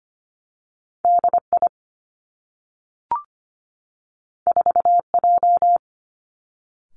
Morse code, heard as a keyed tone, about 25 wpm, 700 Hz. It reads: D I 4 J
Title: Two Challenging Letter-Number Combinations 25wpm